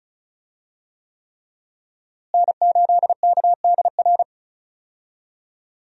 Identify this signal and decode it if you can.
Morse code, heard as a keyed tone, about 36 wpm, 700 Hz. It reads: N8KDR
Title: USA Call Signs - 5 Characters - Repeated 3x - 35wpm